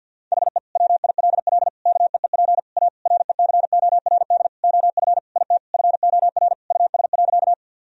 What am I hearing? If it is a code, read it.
Morse code, heard as a keyed tone, about 50 wpm, 700 Hz. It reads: HE WILL KEEP A RECORD OF IT FOR US.
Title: Sentences from Top 1000 Words - Review Part 3 - 50wpm